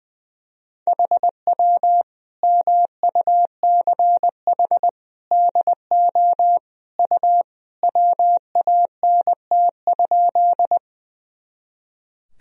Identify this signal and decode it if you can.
Morse code, heard as a keyed tone, about 20 wpm, 700 Hz. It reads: HW MUCH DO U WANT?
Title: Sentences from Top 200 Words and Abbreviations 20wpm